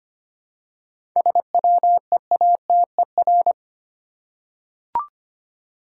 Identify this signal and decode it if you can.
Morse code, heard as a keyed tone, about 25 wpm, 700 Hz. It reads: SWEATER
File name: practice / 25wpm / Easy Words with Hints - 25wpm